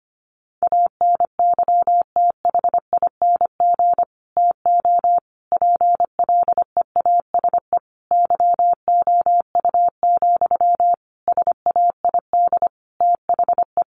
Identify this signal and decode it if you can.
Morse code, heard as a keyed tone, about 25 wpm, 700 Hz. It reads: ANYTHING TO PLEASE YOU, SAID THE
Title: Sentences from Top 1500 Words 25wpm